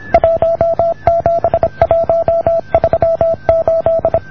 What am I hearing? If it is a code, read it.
Morse code, heard as a keyed tone, about 26 wpm, 620 Hz. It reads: 17138